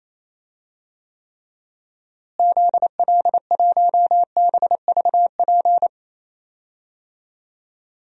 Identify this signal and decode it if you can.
Morse code, heard as a keyed tone, about 28 wpm, 700 Hz. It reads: ZL1BVP
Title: DXCC-100 Call Signs - 6 Characters - Repeated 2x - 28wpm